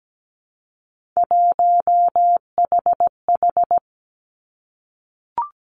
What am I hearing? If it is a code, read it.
Morse code, heard as a keyed tone, about 17 wpm, 700 Hz. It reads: 1HH
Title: Three Challenging Letter-Number Combinations 17wpm